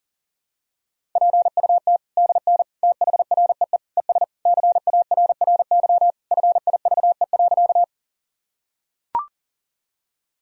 Morse code, heard as a keyed tone, about 40 wpm, 700 Hz. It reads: PUT DN THREE ES CARRY FIVE.
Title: Sentences from Top 400 Words and Abbreviations 40wpm